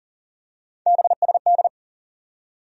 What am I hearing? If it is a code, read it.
Morse code, heard as a keyed tone, about 40 wpm, 700 Hz. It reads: BSD
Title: USA Call Sign Postfixes - 3 Letters 40wpm